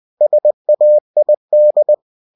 Morse code, heard as a keyed tone, about 20 wpm, 600 Hz. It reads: SAID
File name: cw-books / winnie-the-pooh / winnie-20wpm-page-0105